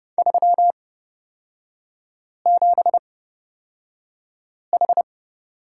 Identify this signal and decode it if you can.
Morse code, heard as a keyed tone, about 30 wpm, spaced as for 5 wpm, 690 Hz. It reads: 37H
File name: course / 30:10wpm / 236 Lesson 38e - Letter q - Three Characters 30wpm